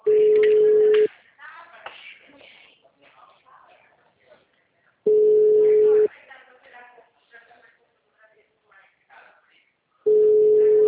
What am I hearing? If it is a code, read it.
Ringback tone (Europe, 425 Hz), with speech.
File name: out- -2010- - - .62